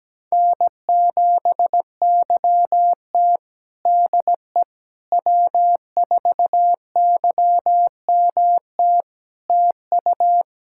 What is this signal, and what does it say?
Morse code, heard as a keyed tone, about 17 wpm, 700 Hz. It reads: N7YT DE W4YMT TU
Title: K1USN SST QSO Practice 17wpm